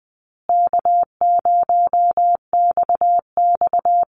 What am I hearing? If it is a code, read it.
Morse code, heard as a keyed tone, about 20 wpm, 700 Hz. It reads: K0XX
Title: Easy Call Signs Repeated 2x - 20wpm